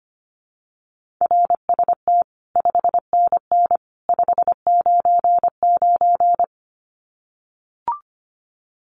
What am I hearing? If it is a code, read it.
Morse code, heard as a keyed tone, about 25 wpm, 700 Hz. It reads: RST 5NN 599
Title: Common QSO Phrases 25wpm